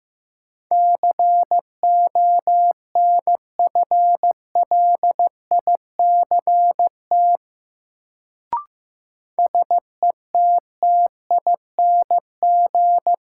Morse code, heard as a keyed tone, about 15 wpm, 700 Hz. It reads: CONFLICT SETTING
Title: Sentences from Top 1300 Words - New Word Drill - 15wpm